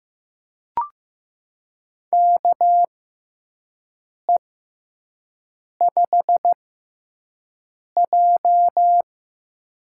Morse code, heard as a keyed tone, about 15 wpm, spaced as for 5 wpm, 700 Hz. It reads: KE5J
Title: USA Call Signs - 4 Characters - 6x Character Spacing - 15wpm